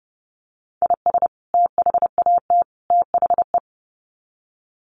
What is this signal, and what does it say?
Morse code, heard as a keyed tone, about 30 wpm, 700 Hz. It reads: IS THAT THE
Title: Sets of 3 Words 30wpm